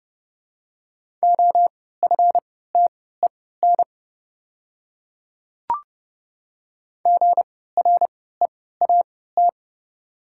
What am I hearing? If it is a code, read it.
Morse code, heard as a keyed tone, about 30 wpm, spaced as for 17 wpm, 700 Hz. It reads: OFTEN GREAT
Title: 3x Character Spacing - Top 300 Words - 30wpm